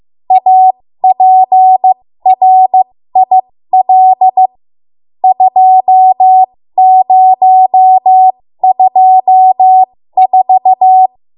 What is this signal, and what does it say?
Morse code, heard as a keyed tone, about 15 wpm, spaced as for 13 wpm, 750 Hz. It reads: APRIL 2024
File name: Morse / 13WPM